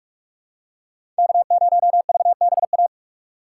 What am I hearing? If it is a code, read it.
Morse code, heard as a keyed tone, about 44 wpm, 700 Hz. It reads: K0VBA
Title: USA Call Signs - 5 Characters - 6x Character Spacing - 45wpm